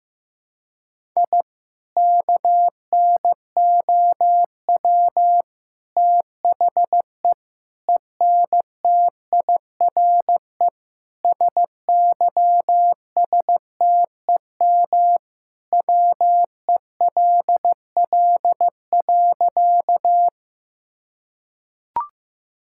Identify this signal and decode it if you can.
Morse code, heard as a keyed tone, about 15 wpm, 700 Hz. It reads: I KNOW THE ENTIRE SYSTEM WELL.
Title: Sentences from Top 1000 Words - Review Part 5 - 15wpm